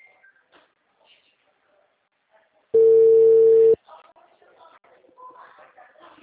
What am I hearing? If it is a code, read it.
Ringback tone (China, 450 Hz), with speech.
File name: out- -2005- - - .78911